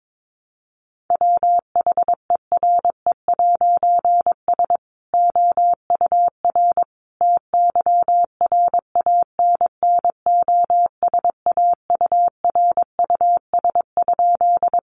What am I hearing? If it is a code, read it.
Morse code, heard as a keyed tone, about 22 wpm, 700 Hz. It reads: WHERE'S OUR TYRANNOSAURUS?
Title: A Sound Of Thunder 22wpm